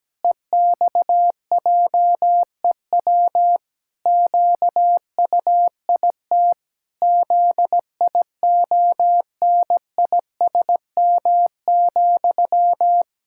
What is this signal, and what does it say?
Morse code, heard as a keyed tone, about 17 wpm, 700 Hz. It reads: EXJEW QUIT ZIONISM,
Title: Pangram Challenge v2 - 17wpm